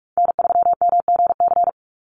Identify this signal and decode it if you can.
Morse code, heard as a keyed tone, about 44 wpm, 700 Hz. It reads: N3MGC
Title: USA Call Signs 45wpm